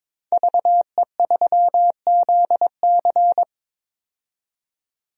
Morse code, heard as a keyed tone, about 22 wpm, 700 Hz. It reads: VE3ZC
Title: DXCC-100 Call Signs - 5 Characters - 22wpm